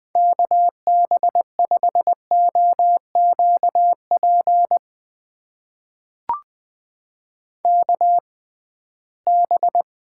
Morse code, heard as a keyed tone, about 20 wpm, spaced as for 15 wpm, 700 Hz. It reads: KB5OQP KB
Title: USA Call Signs - 6 Characters - 6x Character Spacing - 20wpm